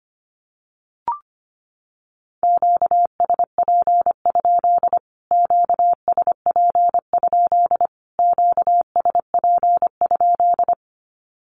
Morse code, heard as a keyed tone, about 25 wpm, 700 Hz. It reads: QSP? QSP? QSP?
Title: Q-Codes Repeated 3x 25wpm